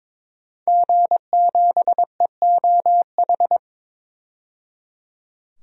Morse code, heard as a keyed tone, about 22 wpm, 700 Hz. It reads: G7EOH
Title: DXCC-100 Call Signs - 5 Characters - 22wpm